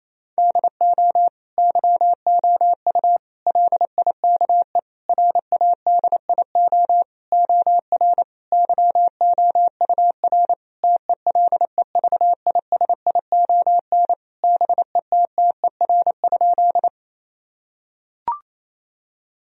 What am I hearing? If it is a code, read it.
Morse code, heard as a keyed tone, about 28 wpm, 700 Hz. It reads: DO YOU LIKE RADIO OR YOUR TELEVISION BETTER?
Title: Sentences from Top 1000 Words - Review Part 4 - 28wpm